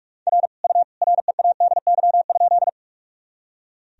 Morse code, heard as a keyed tone, about 46 wpm, 700 Hz. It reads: R U READY?
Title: Sentences from Top 400 Words and Abbreviations 45wpm